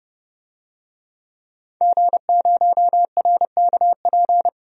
Morse code, heard as a keyed tone, about 30 wpm, 700 Hz. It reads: G0RKP
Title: DXCC-100 Call Signs - 5 Characters - 30wpm